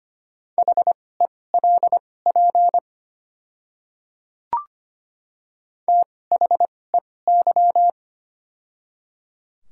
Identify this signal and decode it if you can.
Morse code, heard as a keyed tone, about 25 wpm, spaced as for 18 wpm, 700 Hz. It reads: HELP THEY
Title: 2x Character Spacing - Top 300 Words - 25wpm